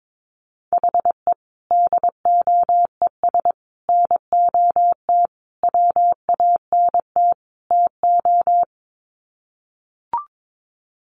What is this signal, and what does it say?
Morse code, heard as a keyed tone, about 22 wpm, 700 Hz. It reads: HE DOES NOT WANT TO
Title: Sets of 5 Words 22wpm